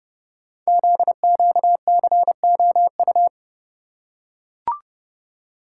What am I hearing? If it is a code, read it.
Morse code, heard as a keyed tone, about 30 wpm, 700 Hz. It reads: ZQCOU
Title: Code Groups - 5 Letters - 30wpm